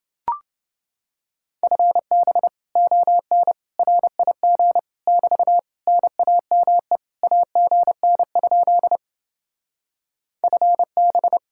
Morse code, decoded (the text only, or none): FB ON RIG <BT> NAME AGN? FB